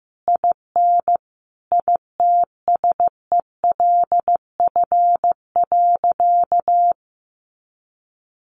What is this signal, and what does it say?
Morse code, heard as a keyed tone, about 15 wpm, 700 Hz. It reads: IN ITSELF.
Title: Sentences from Top 900 Words 15wpm